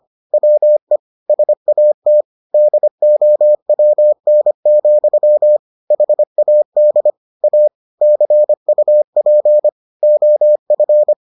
Morse code, heard as a keyed tone, about 25 wpm, 600 Hz. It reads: WE SAT DOWN, HAD A CUP OF